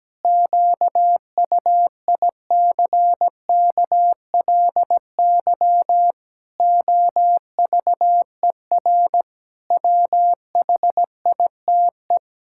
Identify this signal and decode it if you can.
Morse code, heard as a keyed tone, about 17 wpm, 700 Hz. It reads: QUICKLY OVER WHITE
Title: Pangram Challenge v2 - 17wpm